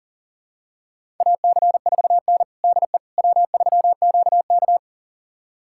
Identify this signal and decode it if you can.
Morse code, heard as a keyed tone, about 40 wpm, 700 Hz. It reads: AC4N DE W3QK
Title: USA Callsign DE Callsign - 4 Characters - 40wpm